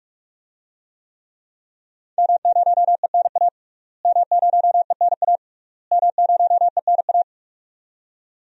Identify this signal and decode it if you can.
Morse code, heard as a keyed tone, about 41 wpm, 700 Hz. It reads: M0ENA M0ENA M0ENA
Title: DXCC-100 Call Signs - 5 Characters - Repeated 3x - 45wpm